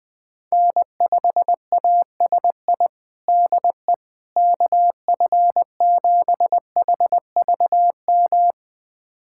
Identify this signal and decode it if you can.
Morse code, heard as a keyed tone, about 20 wpm, 700 Hz. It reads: N5ASI DE KF7HVM
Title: USA Callsign DE Callsign 20wpm